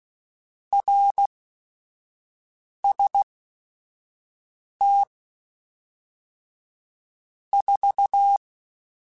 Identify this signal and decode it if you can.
Morse code, heard as a keyed tone, about 16 wpm, spaced as for 5 wpm, 800 Hz. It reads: RST 4
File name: RSTs / RST-mp3s / Your RSTs @5